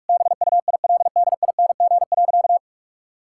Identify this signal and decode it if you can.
Morse code, heard as a keyed tone, about 42 wpm, 700 Hz. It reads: BUILDING.
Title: Sentences from Top 1000 Words - Review Part 6 - 45wpm